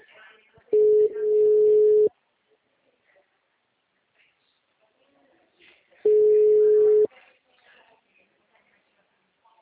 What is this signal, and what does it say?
Ringback tone (Europe, 425 Hz), with speech.
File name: out- -2012- - - .2